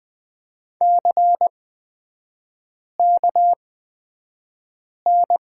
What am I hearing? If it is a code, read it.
Morse code, heard as a keyed tone, about 20 wpm, spaced as for 5 wpm, 690 Hz. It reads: CKN
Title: 229 Lesson 37e - Letter x - Three Characters 20wpm